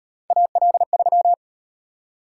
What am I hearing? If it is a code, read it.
Morse code, heard as a keyed tone, about 38 wpm, 700 Hz. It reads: AL3